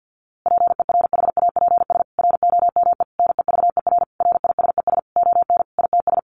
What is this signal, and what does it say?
Morse code, heard as a keyed tone, about 50 wpm, 700 Hz. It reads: PERHAPS ROME NEVER RISES ON ITS
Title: A Sound Of Thunder 50wpm